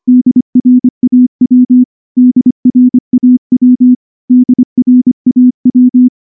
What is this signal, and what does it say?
Morse code, heard as a keyed tone, about 24 wpm, 250 Hz. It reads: DRAW DRAW DRAW
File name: morse-training-25WPM